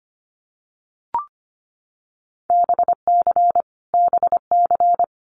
Morse code, heard as a keyed tone, about 25 wpm, 700 Hz. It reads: BC BC
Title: Canadian Provinces and Territories - Repeated 3x - 25wpm